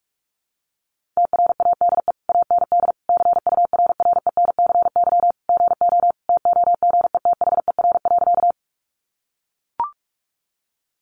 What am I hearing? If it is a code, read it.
Morse code, heard as a keyed tone, about 44 wpm, 700 Hz. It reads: TRADE AND CURRENCY GO TOGETHER.
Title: Sentences from Top 1500 Words 45wpm